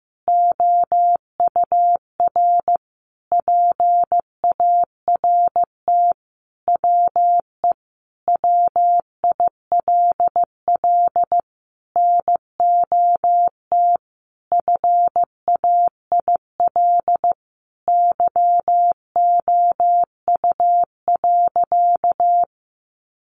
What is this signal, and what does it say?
Morse code, heard as a keyed tone, about 15 wpm, 700 Hz. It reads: OUR PART WE WILL NOT FAIL YOU.
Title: Sentences from Top 1100 Words 15wpm